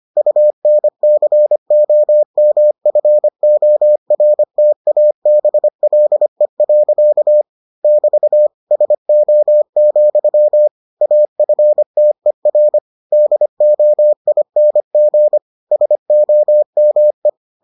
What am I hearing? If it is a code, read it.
Morse code, heard as a keyed tone, about 25 wpm, 600 Hz. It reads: UNCOMFORTABLE. <BT> SO, AFTER DOING SOME